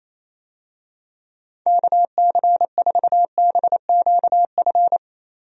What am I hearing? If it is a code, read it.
Morse code, heard as a keyed tone, about 28 wpm, 700 Hz. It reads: KC4BQF
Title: USA Call Signs - 6 Characters - Repeated 3x - 28wpm